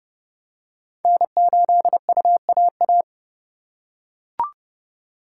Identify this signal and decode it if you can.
Morse code, heard as a keyed tone, about 29 wpm, 700 Hz. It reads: N8UAA